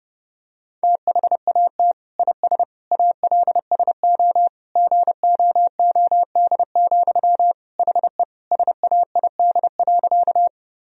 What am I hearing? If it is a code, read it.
Morse code, heard as a keyed tone, about 32 wpm, 700 Hz. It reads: THAT IS ALSO GOOD, HE SAID.
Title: Sentences from Top 1000 Words - Review Part 1 - 30wpm